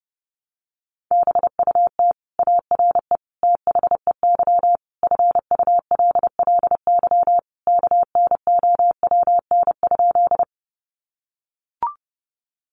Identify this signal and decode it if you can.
Morse code, heard as a keyed tone, about 30 wpm, 700 Hz. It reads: BUT ARE THEY FULLY KNOWN?
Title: Sentences from Top 1300 Words 30wpm